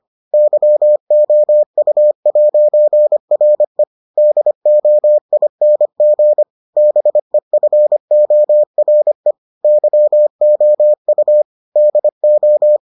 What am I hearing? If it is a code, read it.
Morse code, heard as a keyed tone, about 25 wpm, 600 Hz. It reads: YOU'RE DOING BEFORE YOU DO